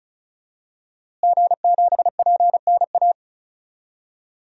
Morse code, heard as a keyed tone, about 36 wpm, 700 Hz. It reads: G7PNA
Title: DXCC-100 Call Signs - 5 Characters - Repeated 3x - 35wpm